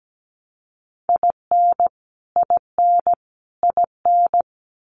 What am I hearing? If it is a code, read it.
Morse code, heard as a keyed tone, about 17 wpm, 700 Hz. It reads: IN IN IN
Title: US States Repeated 3x 17wpm